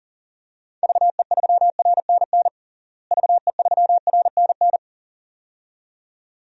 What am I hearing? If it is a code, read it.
Morse code, heard as a keyed tone, about 37 wpm, 700 Hz. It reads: VE3RNN VE3RNN
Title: DXCC-100 Call Signs - 6 Characters - Repeated 2x - 40wpm